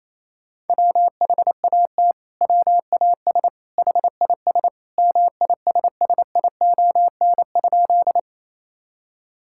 Morse code, heard as a keyed tone, about 28 wpm, 700 Hz. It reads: WHAT WAS HIS MISSION?
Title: Sentences from Top 1600 Words 28wpm